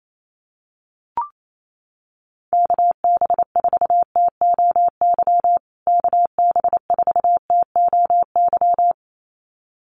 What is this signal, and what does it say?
Morse code, heard as a keyed tone, about 28 wpm, 700 Hz. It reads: KB4TOY KB4TOY
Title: USA Call Signs Repeated 2x 28wpm